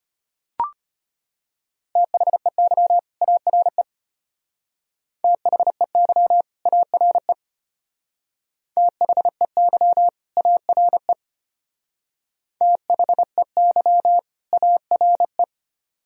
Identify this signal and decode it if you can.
Morse code, heard as a keyed tone, about 33 wpm, 700 Hz. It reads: THEY ARE THEY ARE THEY ARE THEY ARE